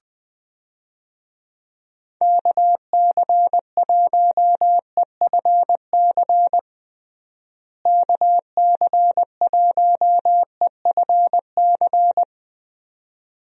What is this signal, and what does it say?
Morse code, heard as a keyed tone, about 20 wpm, 700 Hz. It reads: KC1EFC KC1EFC
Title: USA Call Signs - 6 Characters - Repeated 3x - 20wpm